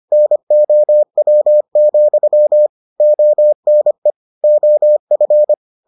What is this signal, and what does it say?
Morse code, heard as a keyed tone, about 25 wpm, 600 Hz. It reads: NOW, ONE OF